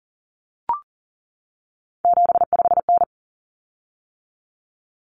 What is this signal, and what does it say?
Morse code, heard as a keyed tone, about 40 wpm, 700 Hz. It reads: 75N
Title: Common Cut Numbers - 3-Digit 40wpm